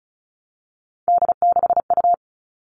Morse code, heard as a keyed tone, about 34 wpm, 700 Hz. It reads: D6U
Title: All Cut Numbers - 3-Digit 35wpm